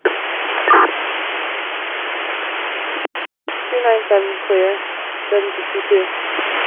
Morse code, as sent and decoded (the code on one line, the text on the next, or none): none